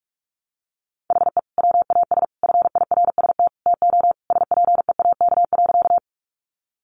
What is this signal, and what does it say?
Morse code, heard as a keyed tone, about 46 wpm, 700 Hz. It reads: HE WAS FIRST TO SPEAK.
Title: Sentences from Top 1100 Words 45wpm